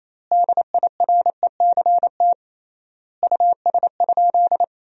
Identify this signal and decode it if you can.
Morse code, heard as a keyed tone, about 28 wpm, 700 Hz. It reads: DIRECT US?
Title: 3x Word-Spacing - Sentences from Top 400 Words - 28wpm